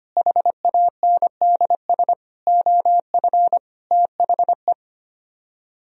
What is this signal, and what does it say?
Morse code, heard as a keyed tone, about 31 wpm, 700 Hz. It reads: HANDS OF THE
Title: Speed-Racing - Sets of 5 Words 25wpm